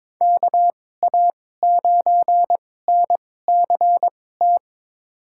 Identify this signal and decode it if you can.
Morse code, heard as a keyed tone, about 22 wpm, spaced as for 16 wpm, 700 Hz. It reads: KA9NCT
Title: USA Call Signs - 6 Characters - 2x Character Spacing - 22wpm